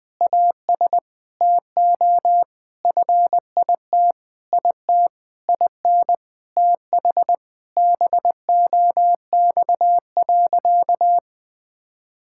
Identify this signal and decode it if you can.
Morse code, heard as a keyed tone, about 20 wpm, 700 Hz. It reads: AS TO FIT IT IN TH BOX.
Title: Sentences with 6-Word Phrases - Spoken Beginning - 20wpm